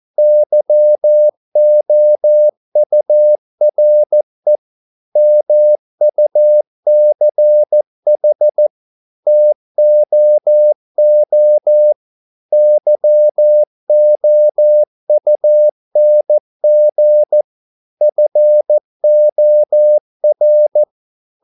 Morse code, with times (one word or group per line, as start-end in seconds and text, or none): 0.2-4.6 YOURE
5.2-8.7 MUCH
9.3-11.9 TOO
12.5-17.4 YOUNG
18.0-20.8 FOR